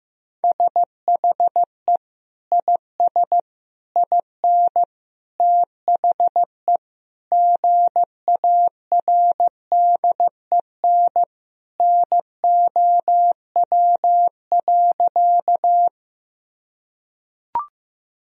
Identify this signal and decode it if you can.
Morse code, heard as a keyed tone, about 15 wpm, 700 Hz. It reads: SHE IS IN THE GARDEN NOW.
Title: Sentences from Top 1000 Words - Review Part 4 - 15wpm